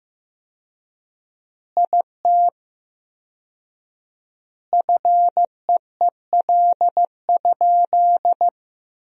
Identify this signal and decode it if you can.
Morse code, heard as a keyed tone, about 15 wpm, 700 Hz. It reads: IT FEEL?
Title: 4x Word-Spacing - Sentences from Top 400 Words - 15wpm